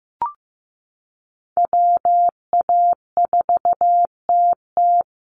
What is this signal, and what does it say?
Morse code, heard as a keyed tone, about 15 wpm, 700 Hz. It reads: WA4TT
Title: Call Signs Repeated 2x - 15wpm